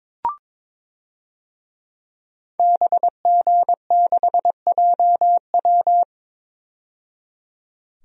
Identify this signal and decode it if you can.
Morse code, heard as a keyed tone, about 22 wpm, 700 Hz. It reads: BG6JW
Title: DXCC-100 Call Signs - 5 Characters - 22wpm